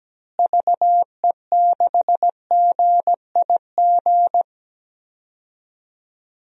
Morse code, heard as a keyed tone, about 17 wpm, 700 Hz. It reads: VE6GIG
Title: DXCC-100 Call Signs - 6 Characters - Repeated 3x - 17wpm